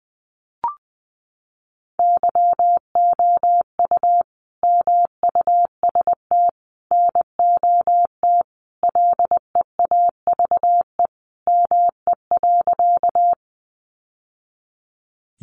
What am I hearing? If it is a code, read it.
Morse code, heard as a keyed tone, about 20 wpm, 700 Hz. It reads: YOU MUST NOT LEAVE ME.
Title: Sentences from Top 600 Words 20wpm